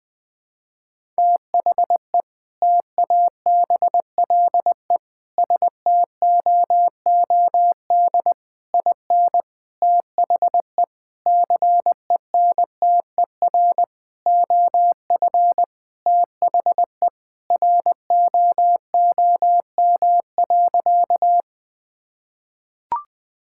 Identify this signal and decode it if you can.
Morse code, heard as a keyed tone, about 20 wpm, 700 Hz. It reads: THE TABLE STOOD IN THE CENTER OF THE ROOM.
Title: Sentences with 4-Word Phrases - Spoken Beginning - 20wpm